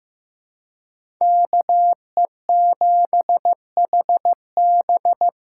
Morse code, heard as a keyed tone, about 15 wpm, 700 Hz. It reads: KE7HB
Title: USA Call Signs - 6 Characters - 8x Character Spacing - 15wpm